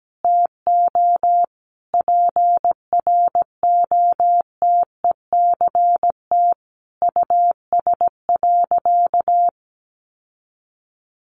Morse code, heard as a keyed tone, about 17 wpm, 700 Hz. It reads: TO PROTECT US.
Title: Sentences from Top 900 Words 17wpm